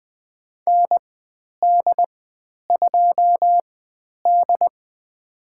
Morse code, heard as a keyed tone, about 20 wpm, spaced as for 10 wpm, 700 Hz. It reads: ND2D
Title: 095 Lesson 16g - Letter c - Call Signs 20wpm